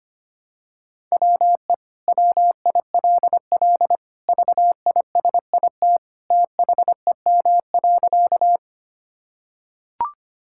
Morse code, heard as a keyed tone, about 25 wpm, 700 Hz. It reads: WE WILL VISIT THEM.
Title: Sentences from Top 1000 Words - Review Part 6 - 25wpm